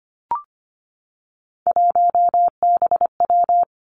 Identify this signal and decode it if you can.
Morse code, heard as a keyed tone, about 25 wpm, 700 Hz. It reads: 1BW
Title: Three Challenging Letter-Number Combinations 25wpm